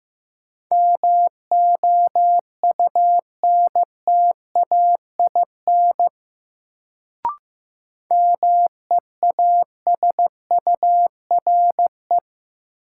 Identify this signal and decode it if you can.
Morse code, heard as a keyed tone, about 15 wpm, 700 Hz. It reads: MOUNTAIN MEASURE
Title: Sentences from Top 1000 Words - Review Part 1 - New Word Drill - 15wpm